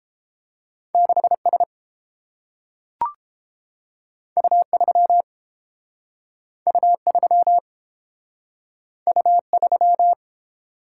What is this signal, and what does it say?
Morse code, heard as a keyed tone, about 30 wpm, 700 Hz. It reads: 6 S U 3 U 3 U 3